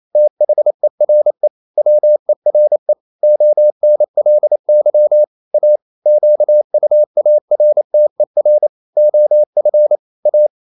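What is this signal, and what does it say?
Morse code, heard as a keyed tone, about 28 wpm, 600 Hz. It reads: THERE WERE ONLY A QUARTER OF A
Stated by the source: Morse code practice at 28 wpm, text of Ken Follett's novel Never in English, from CW med Gnister.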